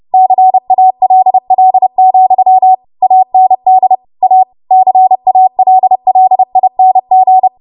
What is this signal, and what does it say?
Morse code, heard as a keyed tone, about 30 wpm, 750 Hz. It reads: CALL, AND A CALLING
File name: Morse / 30WPM